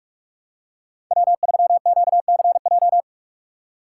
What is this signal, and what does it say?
Morse code, heard as a keyed tone, about 48 wpm, 700 Hz. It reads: W3QCJ
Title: USA Call Signs - 5 Characters - 2x Character Spacing - 45wpm